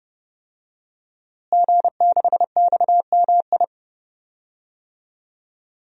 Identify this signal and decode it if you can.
Morse code, heard as a keyed tone, about 30 wpm, 700 Hz. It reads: G6XMI
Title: DXCC-100 Call Signs - 5 Characters - Repeated 2x - 30wpm